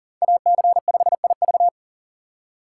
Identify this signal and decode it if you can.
Morse code, heard as a keyed tone, about 40 wpm, 690 Hz. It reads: AC5IV
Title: USA Call Signs - 6x Character Spacing - 40wpm